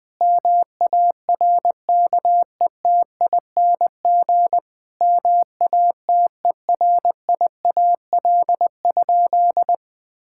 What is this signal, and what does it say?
Morse code, heard as a keyed tone, about 20 wpm, 700 Hz. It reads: MARKETING MATERIAL?
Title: Sentences from Top 1000 Words - Review Part 5 - 20wpm